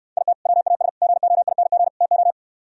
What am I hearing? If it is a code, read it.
Morse code, heard as a keyed tone, about 68 wpm, 700 Hz. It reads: IT WAS GOING TO